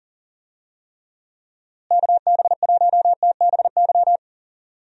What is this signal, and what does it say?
Morse code, heard as a keyed tone, about 39 wpm, 700 Hz. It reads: KB1TBY
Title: USA Call Signs - 6 Characters - Repeated 3x - 40wpm